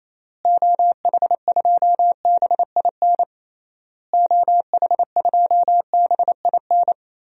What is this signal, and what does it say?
Morse code, heard as a keyed tone, about 28 wpm, 700 Hz. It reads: OH2BIN OH2BIN
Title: DXCC-100 Call Signs - 6 Characters - Repeated 3x - 28wpm